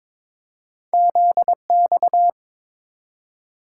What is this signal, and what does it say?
Morse code, heard as a keyed tone, about 22 wpm, 700 Hz. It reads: ZX
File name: practice / 22wpm / Two Letter-Number Combinations - 22wpm